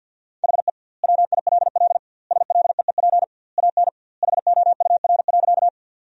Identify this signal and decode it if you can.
Morse code, heard as a keyed tone, about 50 wpm, 700 Hz. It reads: HE WILL SLEEP AN HOUR.
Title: 2x Word-Spacing - Sentences from Top 500 Words - 50wpm